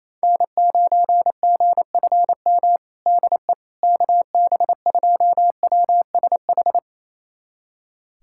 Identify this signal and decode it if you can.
Morse code, heard as a keyed tone, about 28 wpm, 700 Hz. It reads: N9GFM DE KB2WSH
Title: USA Callsign DE Callsign 28wpm